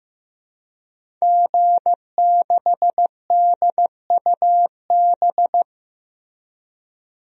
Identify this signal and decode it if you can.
Morse code, heard as a keyed tone, about 15 wpm, 700 Hz. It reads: G6DUB
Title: DXCC-100 Call Signs - 5 Characters - Repeated 2x - 15wpm